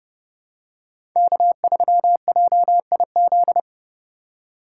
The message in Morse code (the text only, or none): K3JIZ